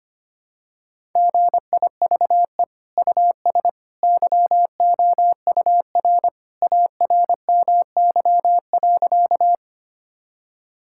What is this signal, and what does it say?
Morse code, heard as a keyed tone, about 25 wpm, 700 Hz. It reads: GIVE US YOUR ARMY.
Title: Sentences from Top 1000 Words - Review Part 5 - 25wpm